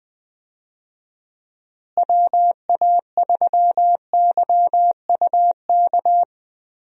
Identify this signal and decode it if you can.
Morse code, heard as a keyed tone, about 21 wpm, 700 Hz. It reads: WA3YUK
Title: USA Call Signs - 6 Characters - Repeated 2x - 20wpm